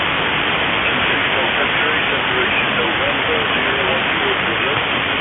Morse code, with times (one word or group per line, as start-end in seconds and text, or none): none